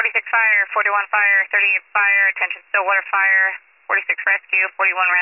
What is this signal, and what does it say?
Emergency Alert System (SAME) tones (header bursts), heard as harsh buzzing bursts, with speech. It is heard from 0.2 to 4.4 s.